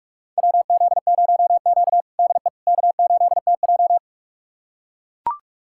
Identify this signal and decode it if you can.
Morse code, heard as a keyed tone, about 42 wpm, 700 Hz. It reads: WZ0Q DE K8TJ
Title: USA Callsign DE Callsign - 4 Characters - 45wpm